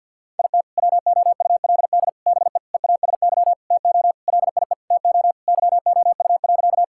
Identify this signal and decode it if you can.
Morse code, heard as a keyed tone, about 50 wpm, 700 Hz. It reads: IT WOULD BE EASY TO LIE TO YOU.
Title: Sentences from Top 1600 Words 50wpm